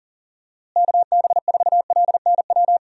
Morse code, heard as a keyed tone, about 40 wpm, 700 Hz. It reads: KB4LNW
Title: USA Call Signs - 6 Characters - Repeated 2x - 40wpm